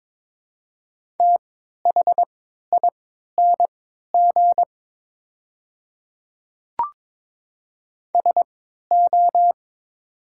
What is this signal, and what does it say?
Morse code, heard as a keyed tone, about 22 wpm, spaced as for 12 wpm, 700 Hz. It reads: THING SO
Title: 3x Character Spacing - Top 100 Words - 22wpm